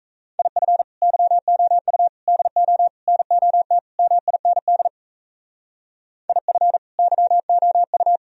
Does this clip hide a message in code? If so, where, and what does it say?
Yes: Morse code from 0.4 to 8.2 s, heard as a keyed tone, about 36 wpm, 700 Hz. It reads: IF YOU DO NOT MIND IF YOU